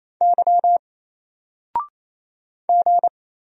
Morse code, heard as a keyed tone, about 27 wpm, 700 Hz. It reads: Y G
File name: Single-Letter-Number-28wpm